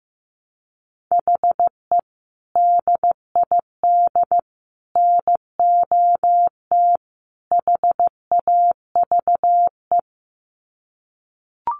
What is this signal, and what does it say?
Morse code, heard as a keyed tone, about 15 wpm, 700 Hz. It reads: HE DID NOT HAVE
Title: Sets of 4 Words 15wpm